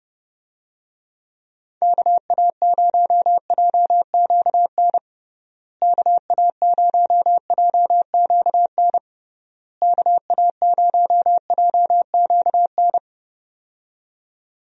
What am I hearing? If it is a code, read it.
Morse code, heard as a keyed tone, about 30 wpm, 700 Hz. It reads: KA0JQN KA0JQN KA0JQN
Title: USA Call Signs - 6 Characters - Repeated 3x - 30wpm